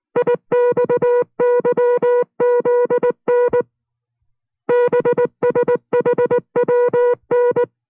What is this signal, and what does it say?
Morse code, heard as a keyed tone, about 19 wpm, 490 Hz. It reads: IXYZN BSHWN